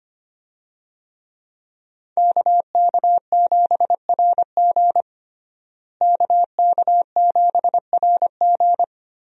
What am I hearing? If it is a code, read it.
Morse code, heard as a keyed tone, about 25 wpm, 700 Hz. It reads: KK7RG KK7RG
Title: USA Call Signs - 5 Characters - Repeated 3x - 25wpm